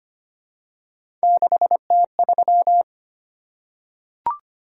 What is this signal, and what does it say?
Morse code, heard as a keyed tone, about 25 wpm, 700 Hz. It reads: 6T3
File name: Three Letter-Number Combinations - 25wpm